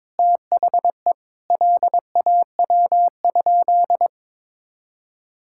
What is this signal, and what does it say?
Morse code, heard as a keyed tone, about 22 wpm, 700 Hz. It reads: THE LAW?
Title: Sentences from Top 1600 Words 22wpm